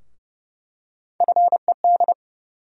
Morse code, heard as a keyed tone, about 30 wpm, 700 Hz. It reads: FED
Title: USA Call Sign Postfixes - 3 Letters 30wpm